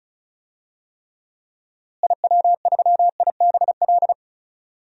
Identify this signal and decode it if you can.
Morse code, heard as a keyed tone, about 36 wpm, 700 Hz. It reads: IW3IBL
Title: DXCC-100 Call Signs - 6 Characters - Repeated 3x - 35wpm